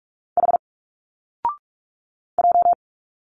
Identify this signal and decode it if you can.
Morse code, heard as a keyed tone, about 45 wpm, 700 Hz. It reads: H J
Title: Single Letters 45wpm